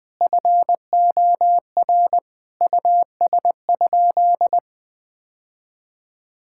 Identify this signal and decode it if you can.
Morse code, heard as a keyed tone, about 20 wpm, 700 Hz. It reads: FOR US?
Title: Sentences from Top 1600 Words 20wpm